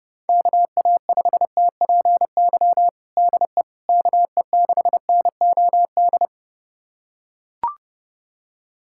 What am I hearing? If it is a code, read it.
Morse code, heard as a keyed tone, about 32 wpm, 700 Hz. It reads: KA5TPY DE KE6NOD
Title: USA Callsign DE Callsign - 6 Characters - 30wpm